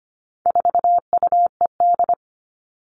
Morse code, heard as a keyed tone, about 25 wpm, 700 Hz. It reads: 4UED
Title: All Cut Numbers - 4-Digit 25wpm